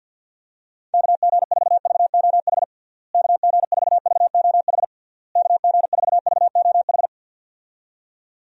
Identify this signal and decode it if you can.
Morse code, heard as a keyed tone, about 49 wpm, 700 Hz. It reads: KG4VOH KG4VOH KG4VOH
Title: USA Call Signs - 6 Characters - Repeated 3x - 50wpm